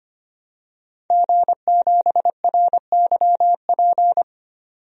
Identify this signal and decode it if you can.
Morse code, heard as a keyed tone, about 25 wpm, 700 Hz. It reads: G7RYP
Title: DXCC-100 Call Signs - 5 Characters - 25wpm